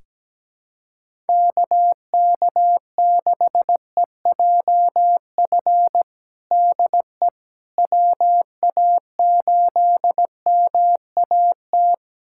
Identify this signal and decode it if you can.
Morse code, heard as a keyed tone, about 17 wpm, 700 Hz. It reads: KK6EJF DE WA8MAT
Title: USA Callsign DE Callsign 17wpm